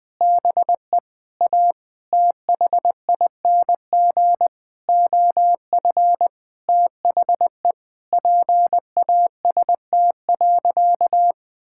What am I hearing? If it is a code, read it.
Morse code, heard as a keyed tone, about 20 wpm, 700 Hz. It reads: BE A THING OF THE PAST.
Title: Sentences with 6-Word Phrases - 20wpm